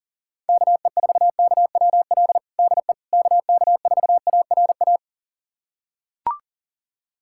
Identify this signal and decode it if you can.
Morse code, heard as a keyed tone, about 40 wpm, 700 Hz. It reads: KE4KWL DE KK4ARA
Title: USA Callsign DE Callsign - 6 Characters - 40wpm